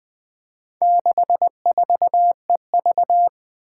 Morse code, heard as a keyed tone, about 21 wpm, 700 Hz. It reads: 64EV
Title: Four Challenging Letter-Number Combinations 20wpm